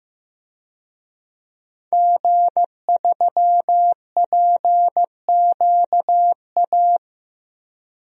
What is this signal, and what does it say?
Morse code, heard as a keyed tone, about 16 wpm, 700 Hz. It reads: G3PQA
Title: DXCC-100 Call Signs - 5 Characters - Repeated 2x - 15wpm